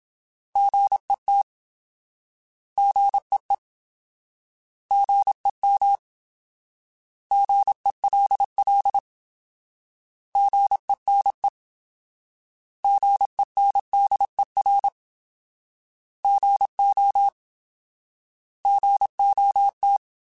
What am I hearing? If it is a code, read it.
Morse code, heard as a keyed tone, about 27 wpm, 800 Hz. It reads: GET GEE GEM GELL GENE GENDER GO GOT
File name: Alphabetical / Alphabetical mp3s / Gs @25